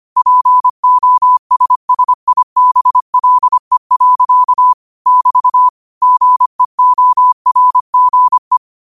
Morse code, heard as a keyed tone, about 25 wpm, 1000 Hz. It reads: POSSIBLE. <BT> GEORGE
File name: qotdfiles / 25wpm